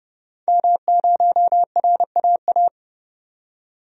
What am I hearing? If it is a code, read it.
Morse code, heard as a keyed tone, about 30 wpm, 690 Hz. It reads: M0RAA
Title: DXCC-100 Call Signs - 5 Characters - 30wpm